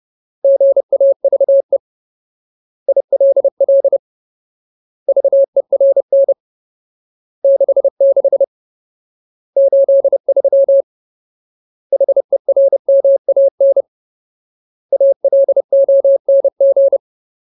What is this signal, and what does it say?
Morse code, heard as a keyed tone, about 30 wpm, 550 Hz. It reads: GAVE ILL VERN 66 83 HERMAN ALONG
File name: Ultimate Word List / 30WPM Ultimate